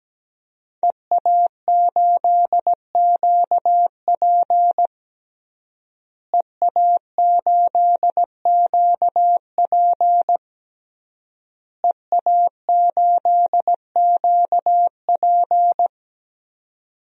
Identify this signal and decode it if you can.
Morse code, heard as a keyed tone, about 17 wpm, 700 Hz. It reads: EA8QP EA8QP EA8QP
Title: DXCC-100 Call Signs - 5 Characters - Repeated 3x - 17wpm